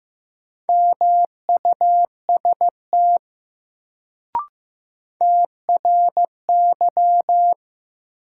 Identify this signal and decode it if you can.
Morse code, heard as a keyed tone, about 15 wpm, 700 Hz. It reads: MUST TRY
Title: Sentences from Top 200 Words - New Word Drill - 15wpm